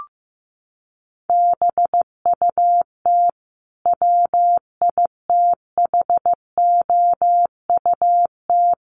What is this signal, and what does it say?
Morse code, heard as a keyed tone, about 15 wpm, 700 Hz. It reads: BUT WITHOUT
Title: Sentences from Top 1400 Words 15wpm